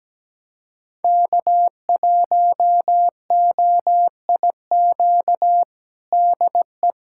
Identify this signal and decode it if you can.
Morse code, heard as a keyed tone, about 17 wpm, 700 Hz. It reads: K1OIQ DE
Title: USA Callsign DE Callsign - 5 Characters - 17wpm